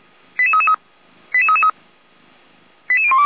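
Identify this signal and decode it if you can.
Five-tone selective call (ZVEI), heard as quick runs of five tones, tones of about 80 ms. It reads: 88333 88333 88012
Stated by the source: ZVEI five-tone selective call 88291 received by SDR.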